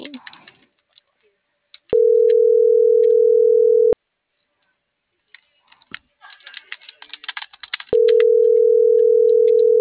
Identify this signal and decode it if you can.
Ringback tone (North America), 440 + 480 Hz.